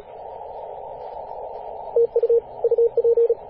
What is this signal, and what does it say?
Morse code, heard as a keyed tone, about 37 wpm, 470 Hz, with radio noise. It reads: TU UP
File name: posts / 2016-apr-22-2153z-kh8-kc0w-18-080mhz-cw